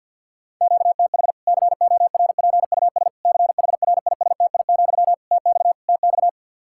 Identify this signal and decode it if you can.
Morse code, heard as a keyed tone, about 51 wpm, 700 Hz. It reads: QTH CORPUS CHRISTI, TX TX
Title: K1USN SST QSO Practice 50wpm